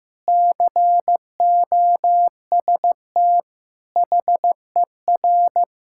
Morse code, heard as a keyed tone, about 14 wpm, 700 Hz. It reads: COST HER
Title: Sentences from Top 1600 Words 15wpm